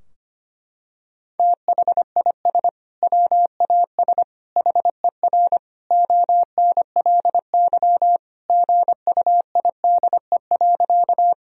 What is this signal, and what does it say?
Morse code, heard as a keyed tone, about 25 wpm, 700 Hz. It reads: THIS WAS HER ONLY GUIDE.
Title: Sentences from Top 1000 Words - Review Part 3 - 25wpm